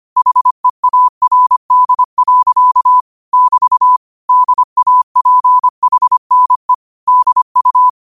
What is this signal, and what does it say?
Morse code, heard as a keyed tone, about 25 wpm, 1000 Hz. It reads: HEARD. <BT> DAPHNE DU